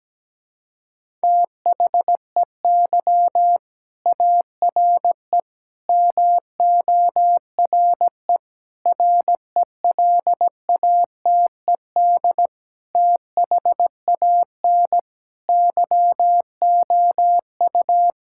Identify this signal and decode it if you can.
Morse code, heard as a keyed tone, about 17 wpm, 700 Hz. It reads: THEY ARE MORE RELATED THAN YOU
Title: Sentences from Top 1000 Words - Review Part 5 - 17wpm